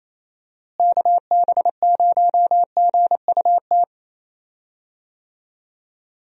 Morse code, heard as a keyed tone, about 27 wpm, 700 Hz. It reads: KB0GUT
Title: USA Call Signs - 6 Characters - Repeated 3x - 28wpm